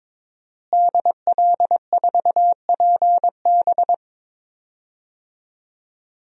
Morse code, heard as a keyed tone, about 22 wpm, 700 Hz. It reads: DL4PB
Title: DXCC-100 Call Signs - 5 Characters - Repeated 2x - 22wpm